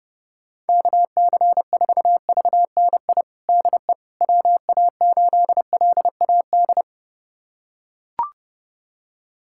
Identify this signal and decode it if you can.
Morse code, heard as a keyed tone, about 30 wpm, 700 Hz. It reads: KC4VNI DE WA8LAD
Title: USA Callsign DE Callsign 30wpm